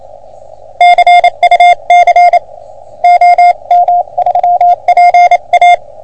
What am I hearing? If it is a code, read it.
Morse code, heard as a keyed tone, about 27 wpm, 710 Hz. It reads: CUC OM3PA
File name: OM3PA PILUP CUC 388